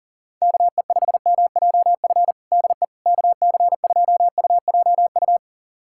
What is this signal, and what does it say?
Morse code, heard as a keyed tone, about 40 wpm, 700 Hz. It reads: KE5MJF DE KC2UJU
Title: USA Callsign DE Callsign - 6 Characters - 40wpm